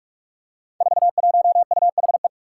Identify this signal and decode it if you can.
Morse code, heard as a keyed tone, about 44 wpm, 700 Hz. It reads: 41UHE
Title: Five Challenging Letter-Number Combinations 45wpm